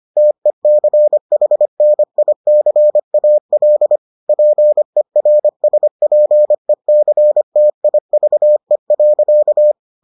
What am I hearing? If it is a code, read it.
Morse code, heard as a keyed tone, about 25 wpm, 600 Hz. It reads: TECHNICAL PERSPECTIVE.